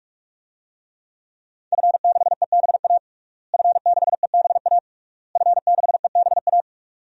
Morse code, heard as a keyed tone, about 41 wpm, 700 Hz. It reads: F6EBA F6EBA F6EBA
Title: DXCC-100 Call Signs - 5 Characters - Repeated 3x - 45wpm